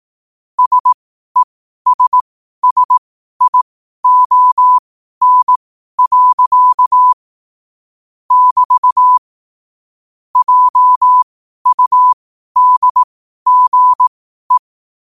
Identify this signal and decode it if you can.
Morse code, heard as a keyed tone, about 18 wpm, spaced as for 12 wpm, 1000 Hz. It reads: SESSION. <BT> JUDGE
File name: qotdfiles / 12wpm